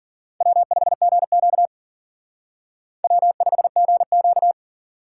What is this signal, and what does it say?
Morse code, heard as a keyed tone, about 51 wpm, 700 Hz. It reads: W5GQ W5GQ